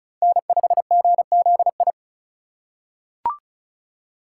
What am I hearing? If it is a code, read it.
Morse code, heard as a keyed tone, about 36 wpm, 700 Hz. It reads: N5GZI
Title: USA Call Signs - 5 Characters - 4x Character Spacing - 35wpm